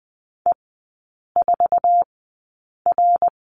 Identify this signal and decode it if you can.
Morse code, heard as a keyed tone, about 20 wpm, spaced as for 8 wpm, 690 Hz. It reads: E4R
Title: 087 Lesson 15e - Number 5 - Three Characters 20wpm